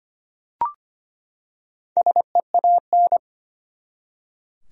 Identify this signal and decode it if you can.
Morse code, heard as a keyed tone, about 25 wpm, 700 Hz. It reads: SEAN
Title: USA Common Names - 25wpm